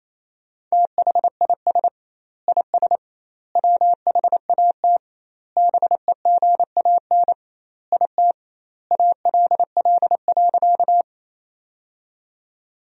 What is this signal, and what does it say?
Morse code, heard as a keyed tone, about 28 wpm, 700 Hz. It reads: THIS IS WHAT BEGAN IT ALL.
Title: 2x Word-Spacing - Sentences from Top 300 Words - 28wpm